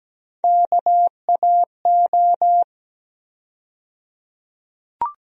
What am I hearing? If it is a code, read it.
Morse code, heard as a keyed tone, about 17 wpm, 700 Hz. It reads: KAO
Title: USA Call Sign Postfixes - 3 Letters 17wpm